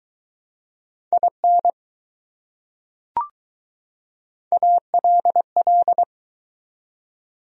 Morse code, heard as a keyed tone, about 20 wpm, 700 Hz. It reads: IN ALL